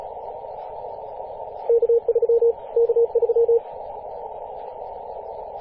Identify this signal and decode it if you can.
Morse code, heard as a keyed tone, about 36 wpm, 470 Hz. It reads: K3 K3
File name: posts / 2016-apr-22-2153z-kh8-kc0w-18-080mhz-cw